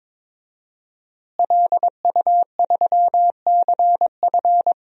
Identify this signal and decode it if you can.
Morse code, heard as a keyed tone, about 21 wpm, 700 Hz. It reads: LU3CF
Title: DXCC-100 Call Signs - 5 Characters - Repeated 3x - 22wpm